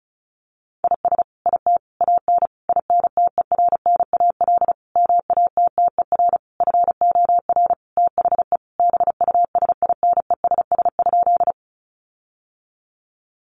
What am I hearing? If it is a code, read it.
Morse code, heard as a keyed tone, about 34 wpm, 700 Hz. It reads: IS IT AN INTERNAL MATTER FOR THE BUSINESS?
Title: Sentences from Top 1400 Words 35wpm